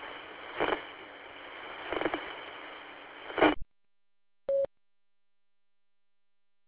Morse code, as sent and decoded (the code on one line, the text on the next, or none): none